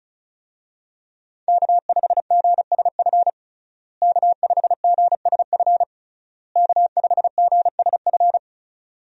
Morse code, heard as a keyed tone, about 34 wpm, 700 Hz. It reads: K5GSF K5GSF K5GSF
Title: USA Call Signs - 5 Characters - Repeated 3x - 35wpm